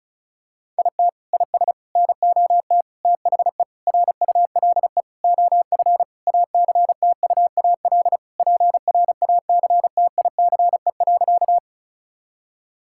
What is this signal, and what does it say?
Morse code, heard as a keyed tone, about 34 wpm, 700 Hz. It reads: IT IS NOT THE RULE OF ACTUAL PRACTICE.
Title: Sentences from Top 1000 Words - Review Part 5 - 35wpm